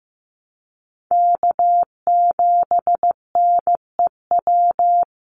Morse code, heard as a keyed tone, about 15 wpm, 700 Hz. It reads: K7NEW
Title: NA SOTA Chaser Callsigns 15wpm